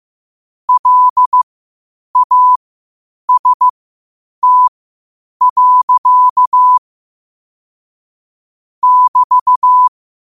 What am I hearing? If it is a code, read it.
Morse code, heard as a keyed tone, about 15 wpm, spaced as for 8 wpm, 1000 Hz. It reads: LAST. <BT>
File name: qotdfiles / 8wpm